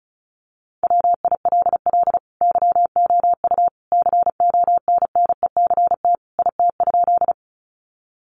Morse code, heard as a keyed tone, about 34 wpm, 700 Hz. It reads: WILL YOU CONNECT IT?
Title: Sentences from Top 1500 Words 35wpm